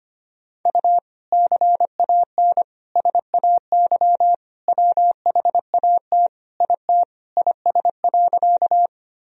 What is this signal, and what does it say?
Morse code, heard as a keyed tone, about 25 wpm, 700 Hz. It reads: U CAN SAY WHAT IT IS.
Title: Sentences from Top 200 Words and Abbreviations 25wpm